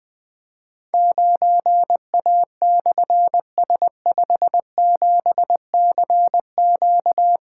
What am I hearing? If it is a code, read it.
Morse code, heard as a keyed tone, about 20 wpm, 700 Hz. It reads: 9A/S57CQ
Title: Difficult Call Signs 20wpm